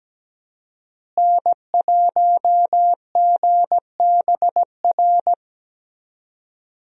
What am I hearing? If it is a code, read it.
Morse code, heard as a keyed tone, about 17 wpm, 700 Hz. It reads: N1GBR
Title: USA Call Signs - 5 Characters - Repeated 3x - 17wpm